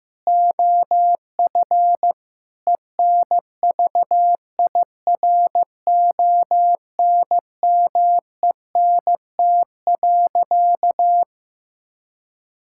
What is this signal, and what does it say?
Morse code, heard as a keyed tone, about 15 wpm, 700 Hz. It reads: OF ENVIRONMENT.
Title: Sentences from Top 1000 Words - Review Part 5 - 15wpm